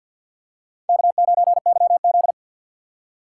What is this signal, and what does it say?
Morse code, heard as a keyed tone, about 49 wpm, 700 Hz. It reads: K9YZ
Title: USA Call Signs - 4 Characters - 3x Character Spacing - 50wpm